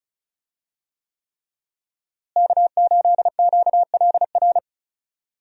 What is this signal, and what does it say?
Morse code, heard as a keyed tone, about 36 wpm, 700 Hz. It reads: K8QLR
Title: USA Call Signs - 5 Characters - Repeated 3x - 35wpm